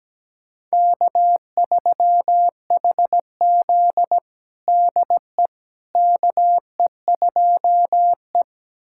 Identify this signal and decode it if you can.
Morse code, heard as a keyed tone, about 17 wpm, 700 Hz. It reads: K3HZ DE KE2E
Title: USA Callsign DE Callsign - 4 Characters - 17wpm